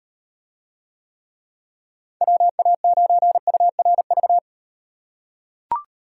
Morse code, heard as a keyed tone, about 38 wpm, 700 Hz. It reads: WA9URV